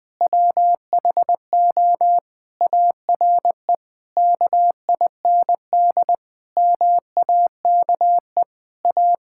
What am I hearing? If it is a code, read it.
Morse code, heard as a keyed tone, about 20 wpm, 700 Hz. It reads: WHO ARE KIND MAKE A
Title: Sentences with 3-Word Phrases - 20wpm